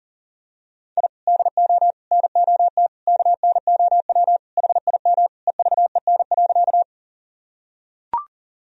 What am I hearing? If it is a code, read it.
Morse code, heard as a keyed tone, about 37 wpm, 700 Hz. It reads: I DO NOT KNOW HIM EVEN.
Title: Sentences from Top 1000 Words - Review Part 1 - 40wpm